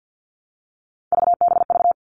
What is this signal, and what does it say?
Morse code, heard as a keyed tone, about 50 wpm, 700 Hz. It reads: VBV
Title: All Cut Numbers - 3-Digit 50wpm